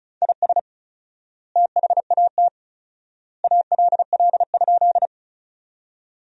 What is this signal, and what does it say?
Morse code, heard as a keyed tone, about 36 wpm, 700 Hz. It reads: IS THAT ALL?
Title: 4x Word-Spacing - Sentences from Top 100 Words - 35wpm